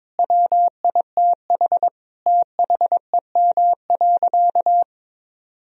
Morse code, heard as a keyed tone, about 21 wpm, 700 Hz. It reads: WITH THEM.
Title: Sentences from Top 1000 Words - Review Part 4 - 22wpm